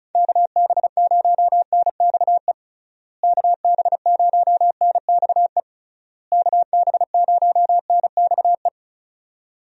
Morse code, heard as a keyed tone, about 36 wpm, 700 Hz. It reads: KB0NXE KB0NXE KB0NXE
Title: USA Call Signs - 6 Characters - Repeated 3x - 35wpm